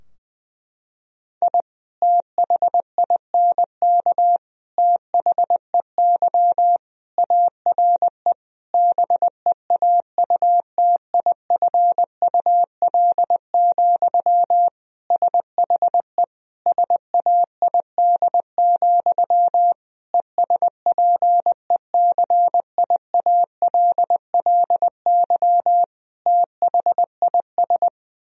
Morse code, heard as a keyed tone, about 20 wpm, 700 Hz. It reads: I THINK THEY ARE BEAUTIFUL, SHE SAID, ESPECIALLY THIS
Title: Sentences from Top 1000 Words - Review Part 4 - 20wpm